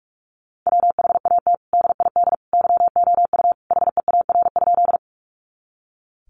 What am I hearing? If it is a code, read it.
Morse code, heard as a keyed tone, about 44 wpm, 700 Hz. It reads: WHAT DID YOU HEAR?
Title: Sentences from Top 300 Words 45wpm